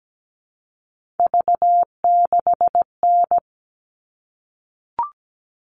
Morse code, heard as a keyed tone, about 17 wpm, 700 Hz. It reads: V6N
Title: All Cut Numbers - 3-Digit 17wpm